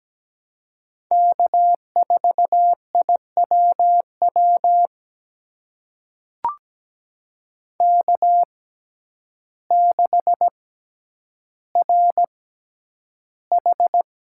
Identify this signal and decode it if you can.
Morse code, heard as a keyed tone, about 17 wpm, spaced as for 9 wpm, 700 Hz. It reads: K4IWW K6RH
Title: USA Call Signs - 5 Characters - 6x Character Spacing - 17wpm